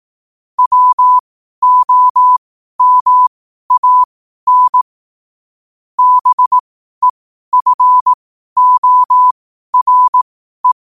Morse code, heard as a keyed tone, about 18 wpm, spaced as for 12 wpm, 1000 Hz. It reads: WOMAN BEFORE
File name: qotdfiles / 12wpm